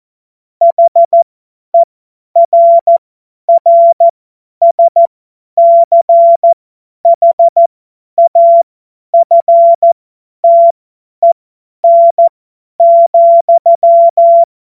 Morse code, heard as a keyed tone, about 14 wpm, spaced as for 10 wpm, 680 Hz. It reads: HERRSCHAFTEN,
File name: Morsekurs / Witze / Witz 20 Februar 2024 50 F70